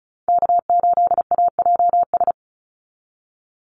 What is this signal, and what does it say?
Morse code, heard as a keyed tone, about 36 wpm, 700 Hz. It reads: K8AJS
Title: Call Signs - 35wpm